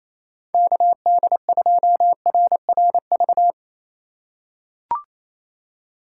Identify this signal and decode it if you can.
Morse code, heard as a keyed tone, about 28 wpm, 700 Hz. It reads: KD2RRV
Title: USA Call Signs - 6 Characters - 6x Character Spacing - 28wpm